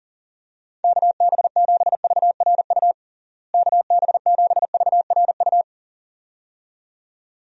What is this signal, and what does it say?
Morse code, heard as a keyed tone, about 40 wpm, 700 Hz. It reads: KB7VRU KB7VRU
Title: USA Call Signs - 6 Characters - Repeated 2x - 40wpm